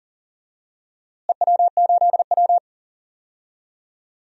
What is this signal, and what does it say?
Morse code, heard as a keyed tone, about 40 wpm, 700 Hz. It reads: EW8W
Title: DXCC-100 Call Signs - 4 Characters - Repeated 3x - 40wpm